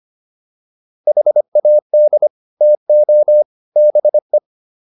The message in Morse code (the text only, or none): HAD TO BE